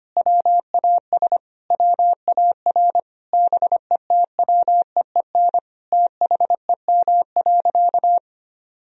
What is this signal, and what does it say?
Morse code, heard as a keyed tone, about 25 wpm, 700 Hz. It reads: WAS WAR BETWEEN THEM.
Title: Sentences from Top 400 Words and Abbreviations 25wpm